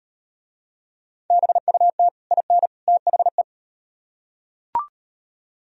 Morse code, heard as a keyed tone, about 38 wpm, 700 Hz. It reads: BUT IN THE